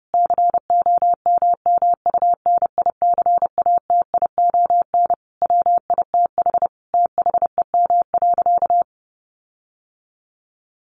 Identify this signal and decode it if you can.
Morse code, heard as a keyed tone, about 28 wpm, 700 Hz. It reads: COMMUNICATION WITH THEM.
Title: Sentences from Top 1400 Words 30wpm